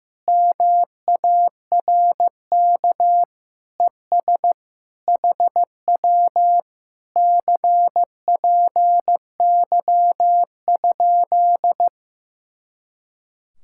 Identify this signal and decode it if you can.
Morse code, heard as a keyed tone, about 15 wpm, 700 Hz. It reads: MARK ES HW CPY?
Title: Speed-Racing Common QSO Phrases Encore - 15wpm